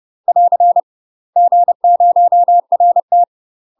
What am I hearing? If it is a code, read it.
Morse code, heard as a keyed tone, about 30 wpm, 700 Hz. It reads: <AR> G0RT
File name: UK MASTER QSOs TRAINING / 30WPM / UK MASTER QSOs 0046